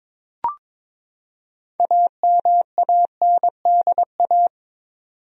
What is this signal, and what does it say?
Morse code, heard as a keyed tone, about 22 wpm, 700 Hz. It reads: AMANDA
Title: USA Common Names - 22wpm